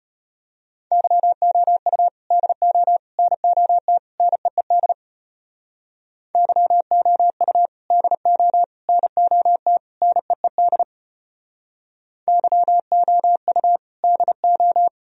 Morse code, heard as a keyed tone, about 33 wpm, 700 Hz. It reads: YOU DO NOT NEED YOU DO NOT NEED YOU DO